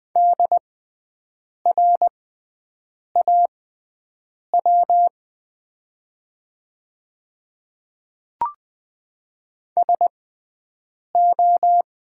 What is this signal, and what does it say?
Morse code, heard as a keyed tone, about 20 wpm, spaced as for 7 wpm, 700 Hz. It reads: DRAW SO
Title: 6x Character Spacing - Top 300 Words - 20wpm